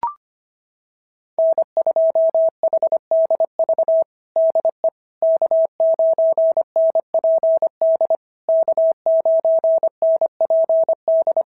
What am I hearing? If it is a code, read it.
Morse code, heard as a keyed tone, about 25 wpm, 650 Hz. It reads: N2HDV DE K9NPD K9NPD
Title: SOTA QSO Practice 25wpm